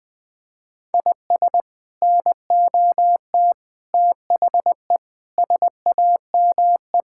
Morse code, heard as a keyed tone, about 20 wpm, 700 Hz. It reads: IS NOT THE SAME
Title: Sets of 6 Words - 20wpm